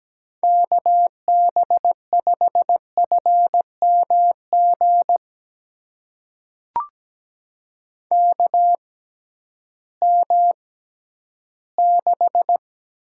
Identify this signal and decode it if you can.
Morse code, heard as a keyed tone, about 17 wpm, spaced as for 11 wpm, 700 Hz. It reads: KB5FMG KM6
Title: USA Call Signs - 6 Characters - 6x Character Spacing - 17wpm